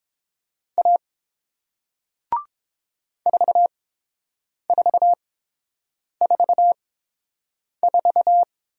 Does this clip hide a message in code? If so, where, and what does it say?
Yes: Morse code from 0.8 to 8.4 s, heard as a keyed tone, about 30 wpm, spaced as for 7 wpm, 700 Hz. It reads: A 4444